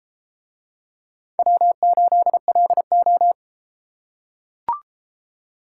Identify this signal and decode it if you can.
Morse code, heard as a keyed tone, about 30 wpm, 700 Hz. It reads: W8LO